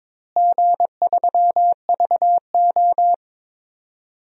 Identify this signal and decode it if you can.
Morse code, heard as a keyed tone, about 21 wpm, 700 Hz. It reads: G3VO
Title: DXCC-100 Call Signs - 4 Characters - Repeated 2x - 22wpm